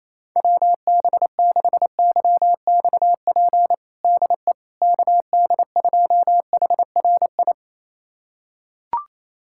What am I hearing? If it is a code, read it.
Morse code, heard as a keyed tone, about 28 wpm, 700 Hz. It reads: WB6YXP DE KD2HRI
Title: USA Callsign DE Callsign 28wpm